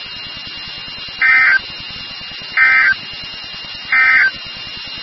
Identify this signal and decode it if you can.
Emergency Alert System (SAME) tones, heard as harsh buzzing bursts. It is heard from 1.1 to 4.4 s.
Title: NOAA Weather Radio WXK36